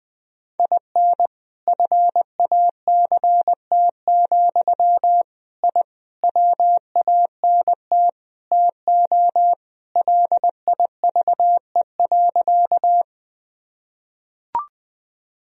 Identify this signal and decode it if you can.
Morse code, heard as a keyed tone, about 20 wpm, 700 Hz. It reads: IN FACT, I WANT TO LIVE.
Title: Sentences from Top 1000 Words - Review Part 3 - 20wpm